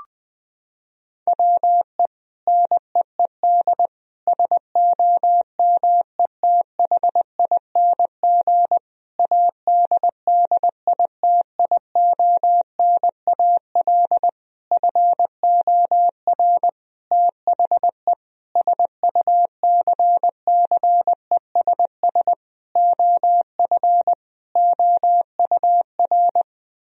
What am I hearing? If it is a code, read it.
Morse code, heard as a keyed tone, about 20 wpm, 700 Hz. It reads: WE NEED SOMETHING ADDITIONAL FOR THE SUCCESS OF OUR
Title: Sentences from Top 1000 Words - Review Part 4 - 20wpm